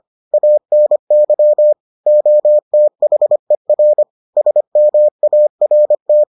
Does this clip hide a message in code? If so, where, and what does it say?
Yes: Morse code from 0.3 to 6.2 s, heard as a keyed tone, about 25 wpm, 600 Hz. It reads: ANY OTHER SMART